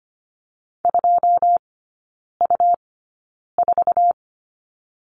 Morse code, heard as a keyed tone, about 25 wpm, spaced as for 9 wpm, 700 Hz. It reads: 2U4
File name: Lesson 17e - Letter u - Three Characters 25wpm